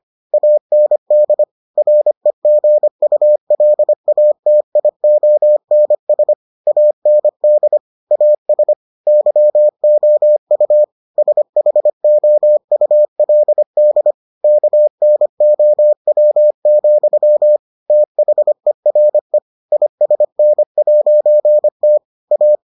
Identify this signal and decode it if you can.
Morse code, heard as a keyed tone, about 25 wpm, 600 Hz. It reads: AND REGULATIONS AND AS YOU SHOULD KNOW, THERE ISN'T A